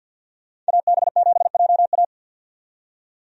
Morse code, heard as a keyed tone, about 52 wpm, 700 Hz. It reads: AB7JA
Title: USA Call Signs - 5 Characters - 50wpm